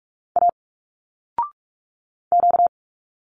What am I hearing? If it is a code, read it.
Morse code, heard as a keyed tone, about 45 wpm, 700 Hz. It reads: A Q
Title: Single Letters 45wpm